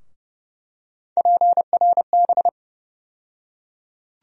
Morse code, heard as a keyed tone, about 30 wpm, 700 Hz. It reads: PRB
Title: USA Call Sign Postfixes - 3 Letters 30wpm